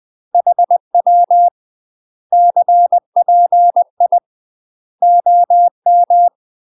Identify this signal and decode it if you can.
Morse code, heard as a keyed tone, about 20 wpm, 700 Hz. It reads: HW CPI OM
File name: UK MASTER QSOs TRAINING / 20WPM / UK MASTER QSOs 0013